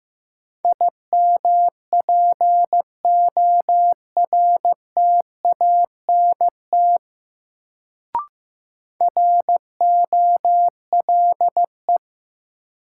Morse code, heard as a keyed tone, about 15 wpm, 700 Hz. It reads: IMPORTANT ROLE
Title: Sets of 3 Words - New Word Drill - 15wpm